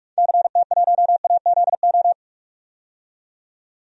Morse code, heard as a keyed tone, about 46 wpm, 700 Hz. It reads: CT1AZO
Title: DXCC-100 Call Signs - 6 Characters - 45wpm